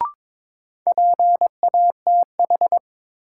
Morse code, heard as a keyed tone, about 22 wpm, 700 Hz. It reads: PATH
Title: Sentences from Top 1400 Words - New Word Drill - 22wpm